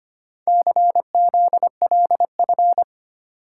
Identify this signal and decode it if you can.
Morse code, heard as a keyed tone, about 25 wpm, 700 Hz. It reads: CZLF
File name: practice / 25wpm / Four Letter-Number Combinations - 25wpm